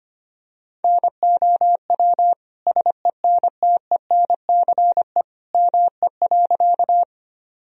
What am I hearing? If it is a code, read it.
Morse code, heard as a keyed tone, about 25 wpm, 700 Hz. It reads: NOW SENTENCE ME.
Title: Sentences from Top 1000 Words - Review Part 1 - 25wpm